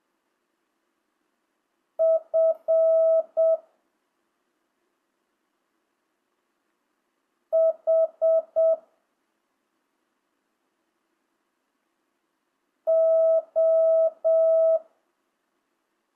Morse code, heard as a keyed tone, about 7 wpm, 650 Hz. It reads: F H O
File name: jh8cbh / 2025cw